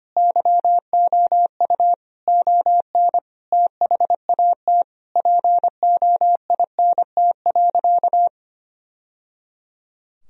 Morse code, heard as a keyed tone, about 25 wpm, 700 Hz. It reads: YOU ON THAT POINT.
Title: Sentences from Top 1000 Words - Review Part 1 - 25wpm